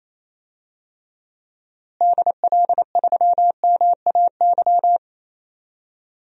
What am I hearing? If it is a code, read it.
Morse code, heard as a keyed tone, about 28 wpm, 700 Hz. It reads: DL3MAY
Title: DXCC-100 Call Signs - 6 Characters - Repeated 3x - 28wpm